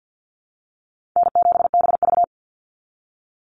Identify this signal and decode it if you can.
Morse code, heard as a keyed tone, about 50 wpm, 700 Hz. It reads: N7BV
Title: Easy Call Signs - 50wpm